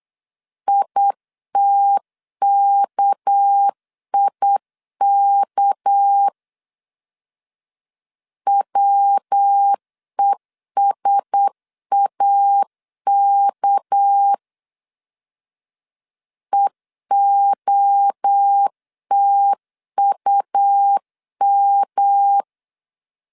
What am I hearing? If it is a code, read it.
Morse code, heard as a keyed tone, about 8 wpm, 800 Hz. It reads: ITKIK WESAK EOTUM
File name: LEZIONI / 6 PKX0 / 6 PK 8wpm 3003